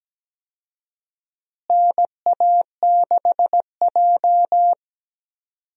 Morse code, heard as a keyed tone, about 17 wpm, 700 Hz. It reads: NA6J
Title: USA Call Signs - 4 Characters - Repeated 3x - 17wpm